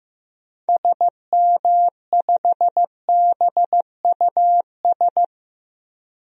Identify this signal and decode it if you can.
Morse code, heard as a keyed tone, about 15 wpm, 700 Hz. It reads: SM5BUS
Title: DXCC-100 Call Signs - 6 Characters - Repeated 2x - 15wpm